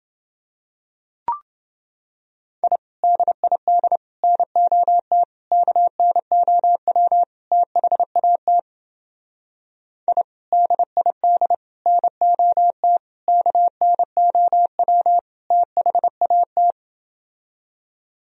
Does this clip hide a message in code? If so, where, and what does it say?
Yes: Morse code from 2.6 to 16.7 s, heard as a keyed tone, about 28 wpm, 700 Hz. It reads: I DID NOT KNOW THAT I DID NOT KNOW THAT